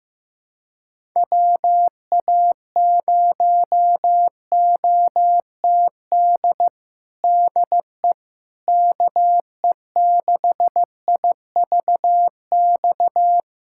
Morse code, heard as a keyed tone, about 15 wpm, 700 Hz. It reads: WA0OTD DE KE6IVX
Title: USA Callsign DE Callsign 15wpm